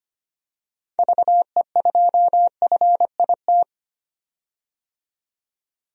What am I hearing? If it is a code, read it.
Morse code, heard as a keyed tone, about 25 wpm, 700 Hz. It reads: VE2FIT
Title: DXCC-100 Call Signs - 6 Characters - 25wpm